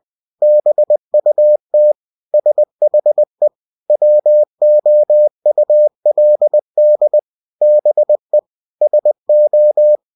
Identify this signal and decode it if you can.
Morse code, heard as a keyed tone, about 20 wpm, 600 Hz. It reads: BUT SHE WOULD BE SO
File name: cw-books / winnie-the-pooh / winnie-20wpm-page-0092